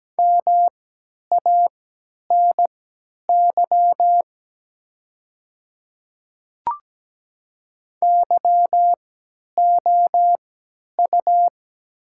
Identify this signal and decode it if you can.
Morse code, heard as a keyed tone, about 17 wpm, spaced as for 10 wpm, 700 Hz. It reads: MANY YOU
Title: 3x Character Spacing - Top 100 Words - 17wpm